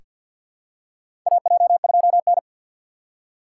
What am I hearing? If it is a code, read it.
Morse code, heard as a keyed tone, about 47 wpm, 690 Hz. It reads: AJ2N
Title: USA Call Signs - 2x Character Spacing - 50wpm